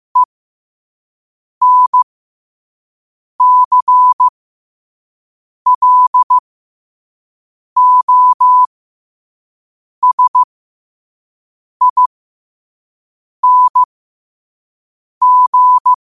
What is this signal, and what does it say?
Morse code, heard as a keyed tone, about 15 wpm, spaced as for 5 wpm, 1000 Hz. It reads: ENCLOSING
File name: qotdfiles / 5wpm